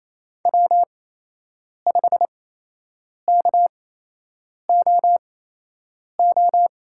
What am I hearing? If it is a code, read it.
Morse code, heard as a keyed tone, about 28 wpm, spaced as for 8 wpm, 700 Hz. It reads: W5KOO
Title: USA Call Signs - 5 Characters - 8x Character Spacing - 28wpm